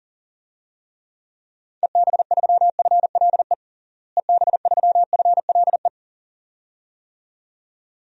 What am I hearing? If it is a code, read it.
Morse code, heard as a keyed tone, about 37 wpm, 700 Hz. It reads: EB3FLE EB3FLE
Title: DXCC-100 Call Signs - 6 Characters - Repeated 2x - 40wpm